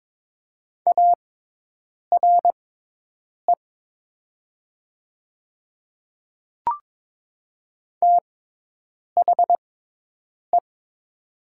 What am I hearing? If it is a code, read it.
Morse code, heard as a keyed tone, about 22 wpm, spaced as for 8 wpm, 700 Hz. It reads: ARE THE
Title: 6x Character Spacing - Top 500 Words - 22wpm